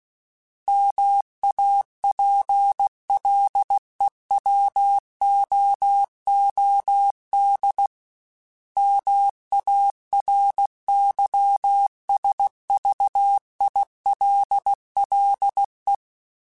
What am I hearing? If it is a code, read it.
Morse code, heard as a keyed tone, about 16 wpm, 800 Hz. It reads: MAPLEWOOD MARYSVILLE
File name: fromSMs / FromSMs mp3s / Cities@15